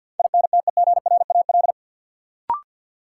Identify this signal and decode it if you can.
Morse code, heard as a keyed tone, about 52 wpm, 700 Hz. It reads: INTEGRAL
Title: Sets of 5 Words - New Word Drill - 50wpm